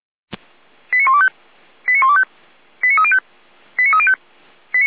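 Five-tone selective call (ZVEI), heard as quick runs of five tones, tones of about 70 ms. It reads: 89125 89125 89305 89305
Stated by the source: ZVEI five-tone selective call received on an SDR.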